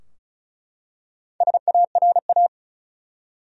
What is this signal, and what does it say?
Morse code, heard as a keyed tone, about 36 wpm, 700 Hz. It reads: SARA
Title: USA Common Names - Max 4 Letters - 35wpm